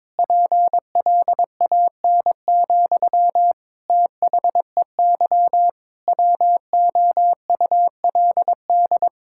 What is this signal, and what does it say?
Morse code, heard as a keyed tone, about 22 wpm, 700 Hz. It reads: PLAN, THEY WOULD
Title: Sentences from Top 1000 Words - Review Part 5 - 22wpm